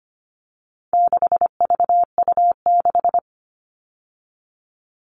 Morse code, heard as a keyed tone, about 25 wpm, 700 Hz. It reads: 6VU6
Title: All Cut Numbers - 4-Digit 25wpm